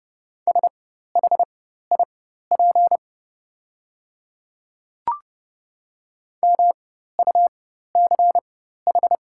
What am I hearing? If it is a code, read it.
Morse code, heard as a keyed tone, about 30 wpm, spaced as for 14 wpm, 690 Hz. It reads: SHIP MUCH
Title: 4x Character Spacing - Top 500 Words - 30wpm